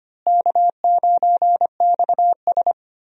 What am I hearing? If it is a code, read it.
Morse code, heard as a keyed tone, about 25 wpm, 700 Hz. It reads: K9XS
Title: USA Call Signs - 4 Characters - Repeated 3x - 25wpm